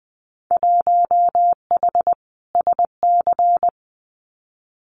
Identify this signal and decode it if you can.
Morse code, heard as a keyed tone, about 20 wpm, 700 Hz. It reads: 1H SC
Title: Winter Field Day Exchanges 20wpm